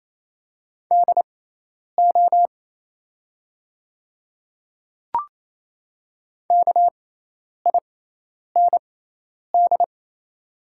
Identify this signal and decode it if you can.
Morse code, heard as a keyed tone, about 28 wpm, spaced as for 10 wpm, 700 Hz. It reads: DO KIND
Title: 6x Character Spacing - Top 200 Words - 28wpm